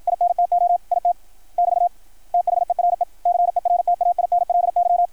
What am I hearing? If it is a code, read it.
Morse code, heard as a keyed tone, about 53 wpm, 660 Hz, with radio noise. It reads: INTO IT <BT> THERE CERTAINLY
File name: archived / CRSnet